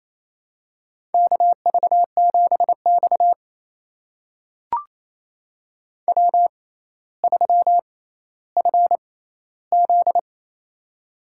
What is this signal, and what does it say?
Morse code, heard as a keyed tone, about 28 wpm, spaced as for 14 wpm, 700 Hz. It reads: KV7X W3FZ
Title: USA Call Signs - 4 Characters - 6x Character Spacing - 28wpm